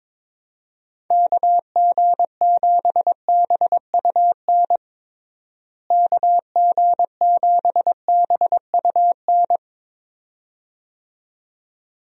Morse code, heard as a keyed tone, about 22 wpm, 700 Hz. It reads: KG7BUN KG7BUN
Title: USA Call Signs - 6 Characters - Repeated 2x - 22wpm